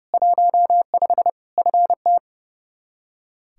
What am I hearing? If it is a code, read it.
Morse code, heard as a keyed tone, about 30 wpm, 700 Hz. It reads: 15 FT
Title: Practice QSO - Exchange 2 - Info - 30wpm